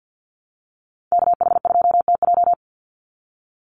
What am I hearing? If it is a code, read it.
Morse code, heard as a keyed tone, about 50 wpm, 700 Hz. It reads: KH2TJ
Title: NA SOTA Activator Callsigns 50wpm